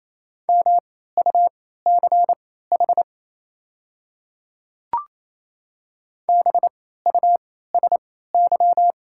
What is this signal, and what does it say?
Morse code, heard as a keyed tone, about 28 wpm, spaced as for 16 wpm, 690 Hz. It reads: MUCH BUSY
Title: 3x Character Spacing - Top 500 Words - 28wpm